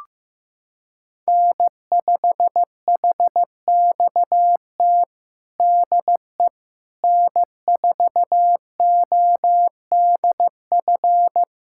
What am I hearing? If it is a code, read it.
Morse code, heard as a keyed tone, about 15 wpm, 700 Hz. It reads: N5HXT DE N4ODF
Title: USA Callsign DE Callsign - 5 Characters - 15wpm